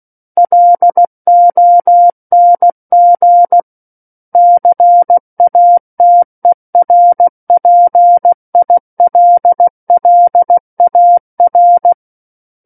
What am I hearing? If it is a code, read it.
Morse code, heard as a keyed tone, about 16 wpm, 700 Hz. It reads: LONG CATERPILLAR
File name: Book One / 07-Chapter 7 - 16 WPM